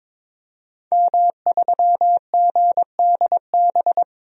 Morse code, heard as a keyed tone, about 22 wpm, 700 Hz. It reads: M3GDB
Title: DXCC-100 Call Signs - 5 Characters - 22wpm